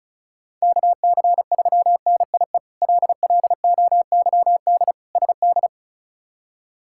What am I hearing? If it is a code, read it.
Morse code, heard as a keyed tone, about 36 wpm, 700 Hz. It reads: KC3NIE LLOYD SD
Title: K1USN SST QSO Practice - 35wpm